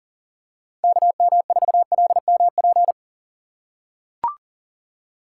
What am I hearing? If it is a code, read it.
Morse code, heard as a keyed tone, about 40 wpm, 700 Hz. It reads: KM4LMP
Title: USA Call Signs - 6 Characters - 3x Character Spacing - 40wpm